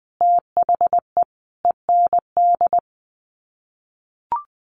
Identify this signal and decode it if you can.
Morse code, heard as a keyed tone, about 19 wpm, 700 Hz. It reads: THE END
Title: Sets of 5 Words Encore2 20wpm